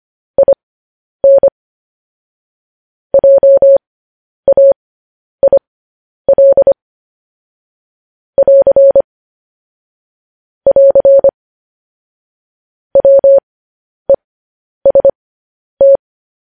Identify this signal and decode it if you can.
Morse code, heard as a keyed tone, about 25 wpm, spaced as for 10 wpm, 550 Hz. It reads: IN JAIL <AR> <AR> WEST
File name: podcast / News headlines-1x-25x10